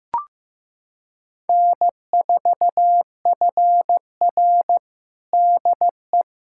Morse code, heard as a keyed tone, about 15 wpm, 700 Hz. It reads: N4FR DE
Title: USA Callsign DE Callsign - 4 Characters - 15wpm